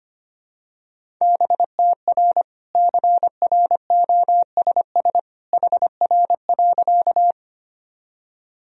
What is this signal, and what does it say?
Morse code, heard as a keyed tone, about 25 wpm, 700 Hz. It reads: BTR CROSS HR.
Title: Sentences from Top 400 Words and Abbreviations 25wpm